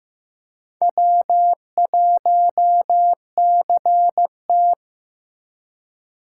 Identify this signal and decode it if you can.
Morse code, heard as a keyed tone, about 18 wpm, 700 Hz. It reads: W1CT
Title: Speed Racing - USA 1x2 Call Signs 15wpm